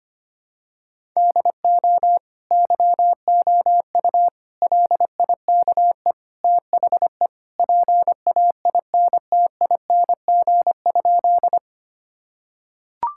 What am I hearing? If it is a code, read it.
Morse code, heard as a keyed tone, about 25 wpm, 700 Hz. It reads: DO YOU LIKE THE PAINTING?
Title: Sentences from Top 1000 Words - Review Part 6 - 25wpm